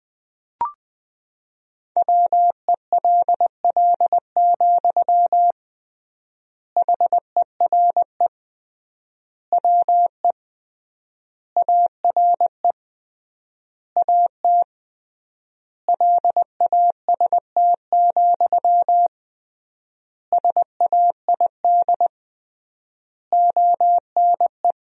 Morse code, heard as a keyed tone, about 21 wpm, 700 Hz. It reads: WELL, HERE WE ARE AT LAST, SAID ONE
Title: 3x Word-Spacing - Sentences from Top 300 Words - 20wpm